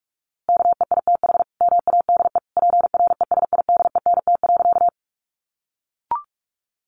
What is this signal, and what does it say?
Morse code, heard as a keyed tone, about 44 wpm, 700 Hz. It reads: KEITH MADE PRESIDENT.
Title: A Sound Of Thunder 45wpm